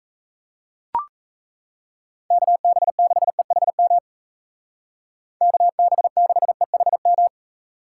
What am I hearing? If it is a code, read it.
Morse code, heard as a keyed tone, about 34 wpm, 700 Hz. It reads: KB6EHM KB6EHM
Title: Speed Racing - USA 2x3 Call Signs 28wpm